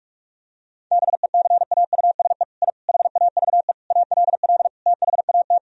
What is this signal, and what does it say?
Morse code, heard as a keyed tone, about 46 wpm, 700 Hz. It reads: BECAUSE I HAVE ALL THAT
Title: Sentences from Top 1000 Words - Review Part 4 - 45wpm